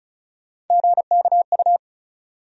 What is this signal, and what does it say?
Morse code, heard as a keyed tone, about 36 wpm, 700 Hz. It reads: GKU
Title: USA Call Sign Postfixes - 3 Letters 35wpm